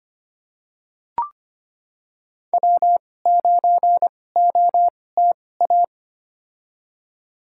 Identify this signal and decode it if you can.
Morse code, heard as a keyed tone, about 25 wpm, spaced as for 18 wpm, 700 Hz. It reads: W9OTA
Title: USA Call Signs - 5 Characters - 2x Character Spacing - 25wpm